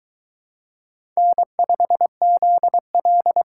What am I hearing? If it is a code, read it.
Morse code, heard as a keyed tone, about 22 wpm, 700 Hz. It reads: N5ZL